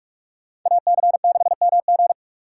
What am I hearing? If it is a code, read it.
Morse code, heard as a keyed tone, about 46 wpm, 700 Hz. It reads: AC6MG
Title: USA Call Signs - 5 Characters - 2x Character Spacing - 45wpm